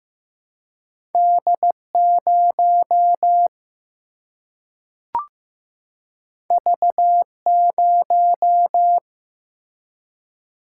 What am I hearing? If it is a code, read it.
Morse code, heard as a keyed tone, about 15 wpm, 700 Hz. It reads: D 0 V 0
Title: Single-Letter Single-Number 15wpm